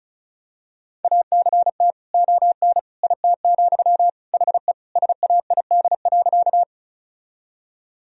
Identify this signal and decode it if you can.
Morse code, heard as a keyed tone, about 36 wpm, 700 Hz. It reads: ACT ON IT, HE SAID.
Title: Sentences from Top 1000 Words - Review Part 1 - 35wpm